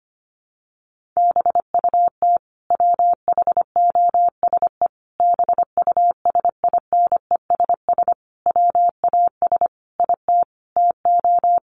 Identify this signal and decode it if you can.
Morse code, heard as a keyed tone, about 25 wpm, 700 Hz. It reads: BUT WHOSE BUSINESS WAS IT TO
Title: Sentences from Top 1500 Words 25wpm